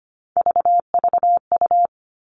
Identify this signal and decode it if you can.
Morse code, heard as a keyed tone, about 25 wpm, 700 Hz. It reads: VVU
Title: All Cut Numbers - 3-Digit 25wpm